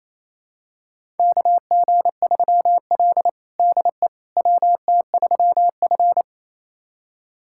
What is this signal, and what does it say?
Morse code, heard as a keyed tone, about 27 wpm, 690 Hz. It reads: KG3L DE WT3F
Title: USA Callsign DE Callsign - 4 Characters - 28wpm